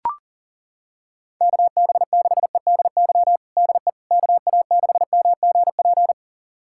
Morse code, heard as a keyed tone, about 40 wpm, 700 Hz. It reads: KB6EDY DE KA6MGP
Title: USA Callsign DE Callsign - 6 Characters - 40wpm